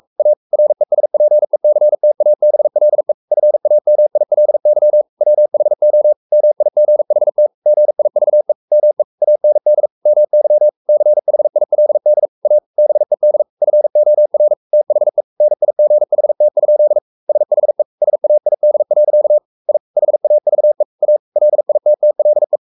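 Morse code, heard as a keyed tone, about 44 wpm, 600 Hz. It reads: A RESPECTABLE FAMILY WHO MIGHT GIVE ME AND MY CHILD A BED FOR THE NIGHT? SHE SAID. I HAVE A LITTLE
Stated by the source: Morse code at 43 wpm sending Ken Follett's novel Never in English, from CW med Gnister.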